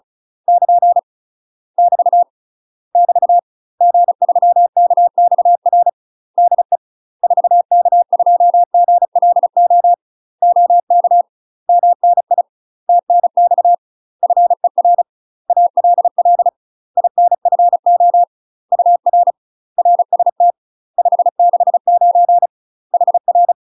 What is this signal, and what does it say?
Morse code, heard as a keyed tone, about 36 wpm, 700 Hz. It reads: <KN> <BT> <BT> G3KXR DE 4K2GLO OK MNI TNX FER ALL INFO UR RST 569 HR
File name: UK MASTER QSOs TRAINING / 35WPM / UK MASTER QSOs 0023